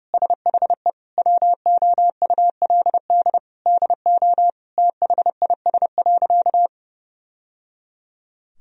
Morse code, heard as a keyed tone, about 32 wpm, 700 Hz. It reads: SHE WOULD DO THIS.
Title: Sentences from Top 1000 Words - Review Part 3 - 30wpm